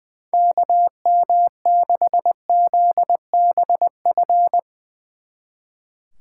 Morse code, heard as a keyed tone, about 22 wpm, 700 Hz. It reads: KM6ZBF
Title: Speed Racing - USA 2x3 Call Signs 20wpm